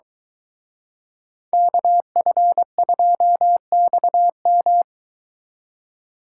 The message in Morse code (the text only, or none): KF2XM